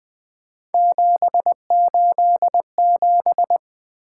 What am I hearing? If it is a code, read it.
Morse code, heard as a keyed tone, about 20 wpm, 700 Hz. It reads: 787
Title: Three Letter-Number Combinations - 20wpm